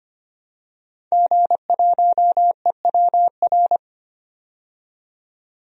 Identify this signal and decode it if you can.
Morse code, heard as a keyed tone, about 25 wpm, 700 Hz. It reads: G1EWR
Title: DXCC-100 Call Signs - 5 Characters - 25wpm